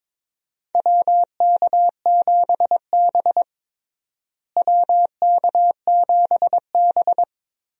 Morse code, heard as a keyed tone, about 22 wpm, 700 Hz. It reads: WK7B WK7B
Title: USA Call Signs - 4 Characters - Repeated 3x - 22wpm